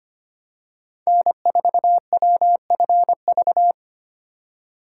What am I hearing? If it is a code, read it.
Morse code, heard as a keyed tone, about 25 wpm, 700 Hz. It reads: N4WFV
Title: USA Call Signs - 4x Character Spacing - 25wpm